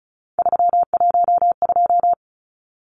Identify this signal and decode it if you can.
Morse code, heard as a keyed tone, about 34 wpm, 700 Hz. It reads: 312
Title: Numbers - 3-Digit 35wpm